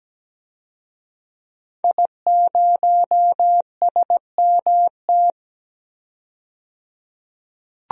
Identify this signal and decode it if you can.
Morse code, heard as a keyed tone, about 17 wpm, 700 Hz. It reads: I0SMT
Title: DXCC-100 Call Signs - 5 Characters - 17wpm